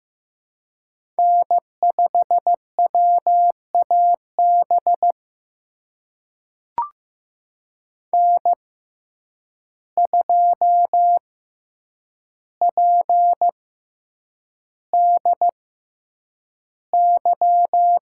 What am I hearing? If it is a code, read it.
Morse code, heard as a keyed tone, about 15 wpm, spaced as for 8 wpm, 700 Hz. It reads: N5WAB N2PDY
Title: USA Call Signs - 5 Characters - 6x Character Spacing - 15wpm